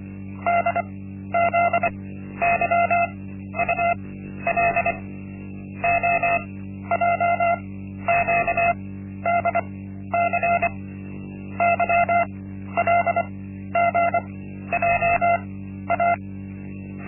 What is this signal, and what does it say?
Morse code, heard as a keyed tone, about 24 wpm, spaced as for 12 wpm, 690 Hz. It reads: DZYUL OJQDC YLGJA
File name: Linksmakalnis-1